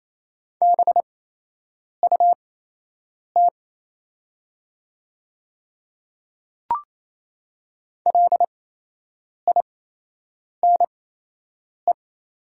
Morse code, heard as a keyed tone, about 28 wpm, spaced as for 8 wpm, 700 Hz. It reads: BUT LINE
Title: 8x Character Spacing - Top 300 Words - 28wpm